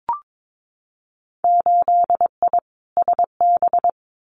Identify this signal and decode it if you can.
Morse code, heard as a keyed tone, about 22 wpm, 700 Hz. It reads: 8I SB
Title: Winter Field Day Exchanges Encore 22wpm